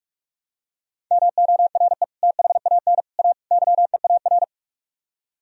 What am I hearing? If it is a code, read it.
Morse code, heard as a keyed tone, about 48 wpm, 700 Hz. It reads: MORE THAN A YEAR